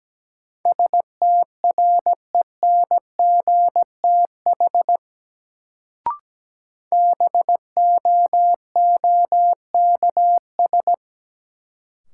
Morse code, heard as a keyed tone, about 17 wpm, 700 Hz. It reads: STRENGTH BOOKS
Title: Sentences from Top 1000 Words - Review Part 4 - New Word Drill - 17wpm